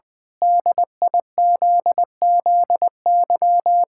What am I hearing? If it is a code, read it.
Morse code, heard as a keyed tone, about 20 wpm, 700 Hz. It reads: DIZZY
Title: Pangram Challenge v2 - 20wpm